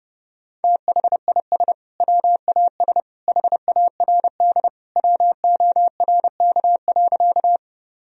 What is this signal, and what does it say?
Morse code, heard as a keyed tone, about 30 wpm, 700 Hz. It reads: THIS WAS HARD WORK.
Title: Sentences from Top 1000 Words - Review Part 1 - 30wpm